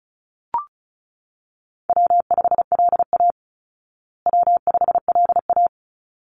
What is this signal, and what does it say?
Morse code, heard as a keyed tone, about 34 wpm, 700 Hz. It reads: W5LA W5LA
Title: Call Signs Repeated 3x - 35wpm